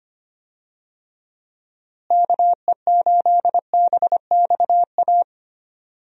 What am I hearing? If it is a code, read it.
Morse code, heard as a keyed tone, about 25 wpm, 700 Hz. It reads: KE8BXA
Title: USA Call Signs - 6 Characters - Repeated 3x - 25wpm